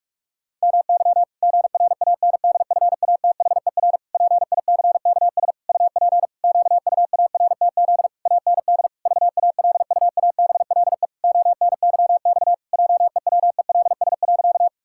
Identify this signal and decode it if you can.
Morse code, heard as a keyed tone, about 46 wpm, 700 Hz. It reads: MY GRANDFATHER PICKS UP QUARTZ AND VALUABLE ONYX JEWELS.
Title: Pangram Challenge v2 - 45wpm